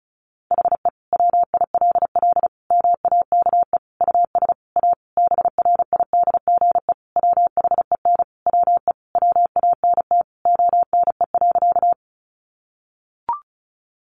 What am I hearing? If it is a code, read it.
Morse code, heard as a keyed tone, about 34 wpm, 700 Hz. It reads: HE WILL MAKE US A BRIDGE WHEN WE WANT ONE.
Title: Sentences from Top 1200 Words 35wpm